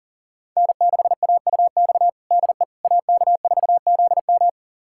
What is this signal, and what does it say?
Morse code, heard as a keyed tone, about 40 wpm, 700 Hz. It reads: N6AUX DE AK4ZM
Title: USA Callsign DE Callsign - 5 Characters - 40wpm